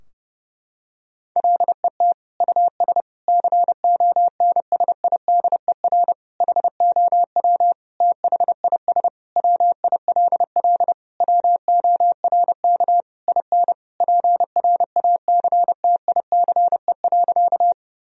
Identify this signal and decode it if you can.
Morse code, heard as a keyed tone, about 30 wpm, 700 Hz. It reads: LET US CONSIDER HOW THIS WILL WORK IN PRACTICE.
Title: Sentences from Top 1000 Words - Review Part 4 - 30wpm